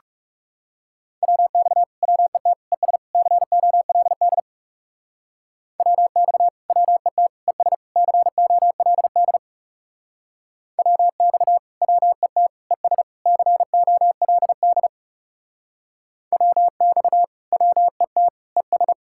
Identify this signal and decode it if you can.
Morse code, heard as a keyed tone, about 37 wpm, 700 Hz. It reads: WX WET ES COLD WX WET ES COLD WX WET ES COLD WX WET ES
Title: Speed-Racing Common QSO Phrases Encore - 30wpm